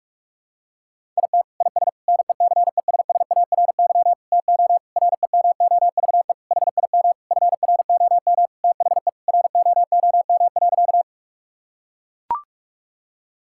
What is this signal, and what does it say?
Morse code, heard as a keyed tone, about 41 wpm, 700 Hz. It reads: IT IS NECESSARY TO REMOVE HIM FROM THE ROOM.
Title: Sentences from Top 1000 Words - Review Part 5 - 45wpm